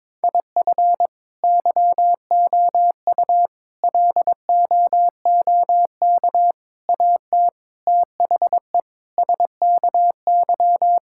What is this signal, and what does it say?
Morse code, heard as a keyed tone, about 22 wpm, 700 Hz. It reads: IF YOU LOOK AT THE SKY
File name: practice / 22wpm / Sentences with 4-Word Phrases - 22wpm